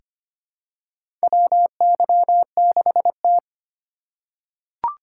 Morse code, heard as a keyed tone, about 25 wpm, 700 Hz. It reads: WY6T
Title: USA Call Signs - 4 Characters - 4x Character Spacing - 25wpm